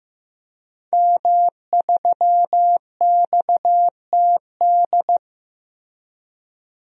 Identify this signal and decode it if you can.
Morse code, heard as a keyed tone, about 15 wpm, 700 Hz. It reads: M3XTD
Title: DXCC-100 Call Signs - 5 Characters - 15wpm